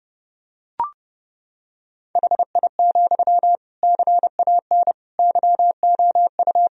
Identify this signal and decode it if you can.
Morse code, heard as a keyed tone, about 32 wpm, 700 Hz. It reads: HI, CAN YOU
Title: Sentences from Top 1000 Words - Review Part 3 - 30wpm